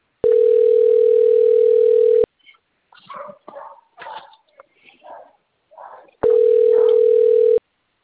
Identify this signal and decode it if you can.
Ringback tone (North America), 440 + 480 Hz.